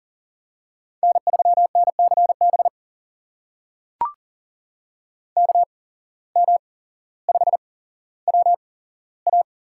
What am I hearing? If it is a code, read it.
Morse code, heard as a keyed tone, about 40 wpm, spaced as for 17 wpm, 700 Hz. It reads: N3NCB KM5WA
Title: USA Call Signs - 5 Characters - 8x Character Spacing - 40wpm